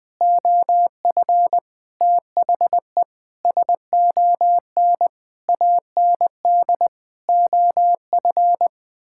Morse code, heard as a keyed tone, about 20 wpm, 700 Hz. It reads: OF THE SON AND OF
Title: Sets of 9 Words - 20wpm